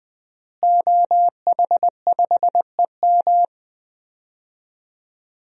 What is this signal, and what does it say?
Morse code, heard as a keyed tone, about 20 wpm, 700 Hz. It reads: OH5EM
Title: DXCC-100 Call Signs - 5 Characters - Repeated 3x - 20wpm